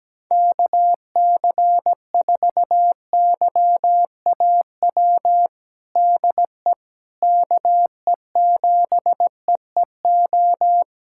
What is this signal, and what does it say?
Morse code, heard as a keyed tone, about 17 wpm, 700 Hz. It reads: KC4YAW DE KE7EEO
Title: USA Callsign DE Callsign 17wpm